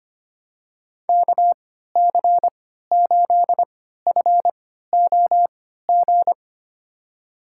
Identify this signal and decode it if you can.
Morse code, heard as a keyed tone, about 25 wpm, spaced as for 14 wpm, 700 Hz. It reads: KC8FOG
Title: USA Call Signs - 6 Characters - 3x Character Spacing - 25wpm